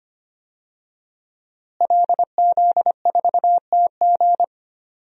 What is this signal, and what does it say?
Morse code, heard as a keyed tone, about 25 wpm, 700 Hz. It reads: LZ4TG
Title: DXCC-100 Call Signs - 5 Characters - Repeated 2x - 25wpm